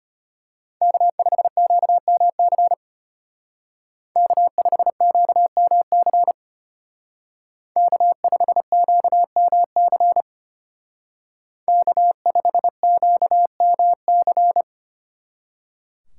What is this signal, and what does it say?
Morse code, heard as a keyed tone, about 33 wpm, 700 Hz. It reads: K5QMC K5QMC K5QMC K5QMC